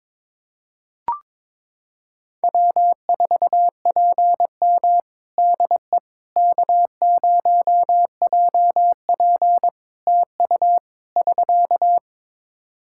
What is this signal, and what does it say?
Morse code, heard as a keyed tone, about 22 wpm, 700 Hz. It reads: W4PM DE K0JP TU <SK>
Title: Practice QSO - Exchange 4 - Ending - 22wpm